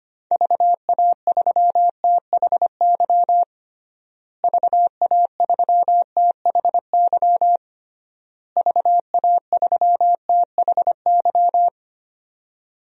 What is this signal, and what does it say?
Morse code, heard as a keyed tone, about 25 wpm, 700 Hz. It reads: VA3THY VA3THY VA3THY
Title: DXCC-100 Call Signs - 6 Characters - Repeated 3x - 25wpm